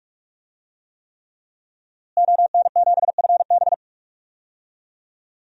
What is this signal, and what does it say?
Morse code, heard as a keyed tone, about 46 wpm, 700 Hz. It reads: ON7FB
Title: DXCC-100 Call Signs - 5 Characters - Repeated 3x - 45wpm